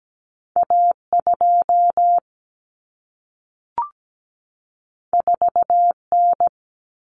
Morse code, heard as a keyed tone, about 17 wpm, 700 Hz. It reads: A2 4N
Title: Common Cut Numbers - 2-Digit 17wpm